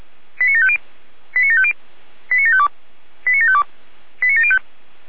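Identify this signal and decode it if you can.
Five-tone selective call (ZVEI), heard as quick runs of five tones, tones of about 70 ms. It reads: 89750 89750 89752 89752 89705
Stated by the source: ZVEI five-tone selective call received on an SDR.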